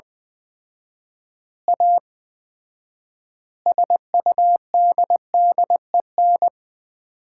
Morse code, heard as a keyed tone, about 21 wpm, 700 Hz. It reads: A SUDDEN
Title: 4x Word-Spacing - Sets of 5 Words - 20wpm